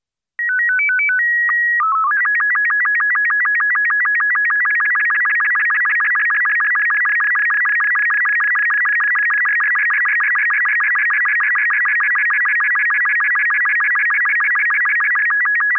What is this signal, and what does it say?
SSTV (slow-scan television), Robot 36.
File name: 14.SSTV-ROBOT36